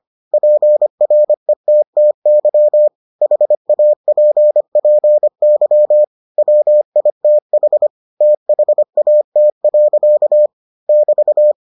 Morse code, heard as a keyed tone, about 25 wpm, 600 Hz. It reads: PRETTY HAPPY WITH THAT. <BT>